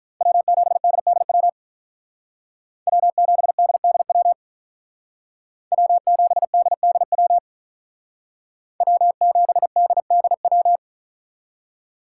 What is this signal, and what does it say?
Morse code, heard as a keyed tone, about 42 wpm, 700 Hz. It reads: W7DDW W7DDW W7DDW W7DDW
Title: Speed Racing - USA 1x3 Call Signs 35wpm